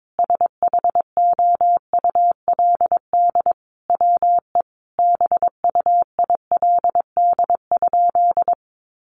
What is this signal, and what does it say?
Morse code, heard as a keyed tone, about 22 wpm, 700 Hz. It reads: SHOULD WE BUILD?
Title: Sentences from Top 1300 Words 22wpm